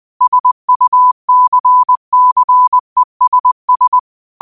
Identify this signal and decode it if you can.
Morse code, heard as a keyed tone, about 20 wpm, 1000 Hz. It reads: SUCCESS
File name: qotdfiles / 20wpm